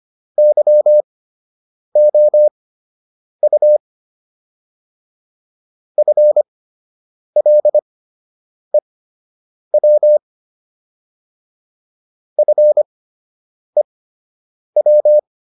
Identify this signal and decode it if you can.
Morse code, heard as a keyed tone, about 25 wpm, spaced as for 8 wpm, 600 Hz. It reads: YOU FLEW FEW